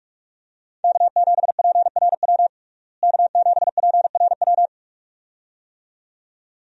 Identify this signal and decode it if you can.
Morse code, heard as a keyed tone, about 46 wpm, 700 Hz. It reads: K7PRW K7PRW
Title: USA Call Signs - 5 Characters - Repeated 2x - 45wpm